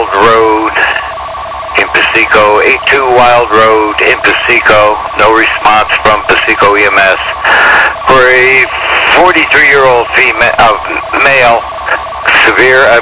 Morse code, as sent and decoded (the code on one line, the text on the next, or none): none